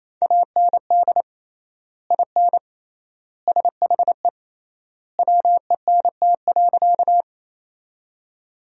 Morse code, heard as a keyed tone, about 27 wpm, 700 Hz. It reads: AND IN SHE WENT.
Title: 3x Word-Spacing - Sentences from Top 200 Words - 28wpm